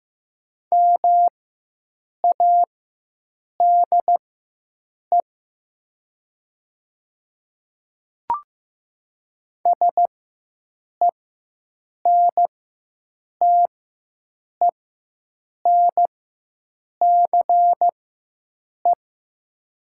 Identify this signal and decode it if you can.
Morse code, heard as a keyed tone, about 15 wpm, spaced as for 7 wpm, 700 Hz. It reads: MADE SENTENCE
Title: 4x Character Spacing - Top 500 Words - 15wpm